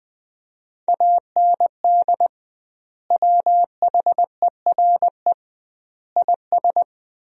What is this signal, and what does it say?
Morse code, heard as a keyed tone, about 20 wpm, 700 Hz. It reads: AND WHERE IS
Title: 2x Word-Spacing - Sentences from Top 200 Words - 20wpm